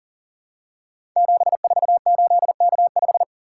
Morse code, heard as a keyed tone, about 39 wpm, 700 Hz. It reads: 748K5
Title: Code Groups - 5 Letter:Numbers - 40wpm